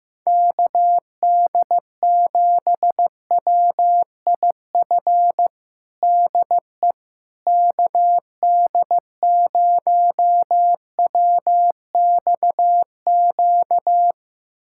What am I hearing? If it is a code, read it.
Morse code, heard as a keyed tone, about 15 wpm, 700 Hz. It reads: KD7WIF DE KD0WXQ
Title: USA Callsign DE Callsign - 6 Characters - 15wpm